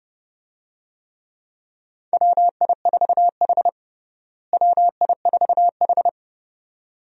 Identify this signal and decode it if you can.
Morse code, heard as a keyed tone, about 32 wpm, 700 Hz. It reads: WI4H WI4H
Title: USA Call Signs - 4 Characters - Repeated 2x - 30wpm